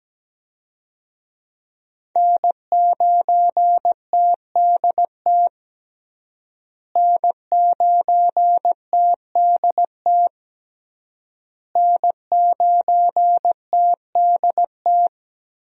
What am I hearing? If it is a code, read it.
Morse code, heard as a keyed tone, about 17 wpm, 700 Hz. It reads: N9TDT N9TDT N9TDT
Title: USA Call Signs - 5 Characters - Repeated 3x - 17wpm